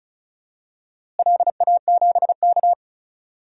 Morse code, heard as a keyed tone, about 34 wpm, 700 Hz. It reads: LA7K
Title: DXCC-100 Call Signs - 4 Characters - Repeated 3x - 35wpm